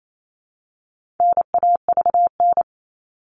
Morse code, heard as a keyed tone, about 28 wpm, 700 Hz. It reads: NAVN
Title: All Cut Numbers - 4-Digit 28wpm